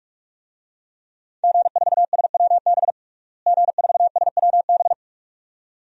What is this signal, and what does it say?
Morse code, heard as a keyed tone, about 42 wpm, 700 Hz. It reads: G4SWB G4SWB
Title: DXCC-100 Call Signs - 5 Characters - Repeated 2x - 45wpm